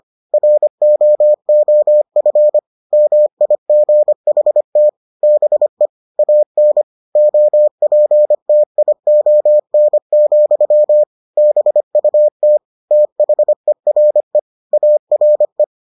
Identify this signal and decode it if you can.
Morse code, heard as a keyed tone, about 25 wpm, 600 Hz. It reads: ROOF MIGHT BE AN OPTION, BUT THERE ARE